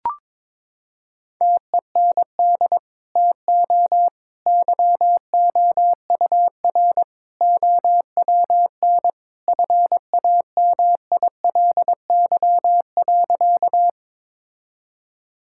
Morse code, heard as a keyed tone, about 22 wpm, 700 Hz. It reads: TEND TO YOUR OWN FAMILY.
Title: Sentences from Top 1000 Words - Review Part 5 - 22wpm